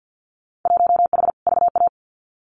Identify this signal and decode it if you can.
Morse code, heard as a keyed tone, about 50 wpm, 700 Hz. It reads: 1H VA
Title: Winter Field Day Exchanges 50wpm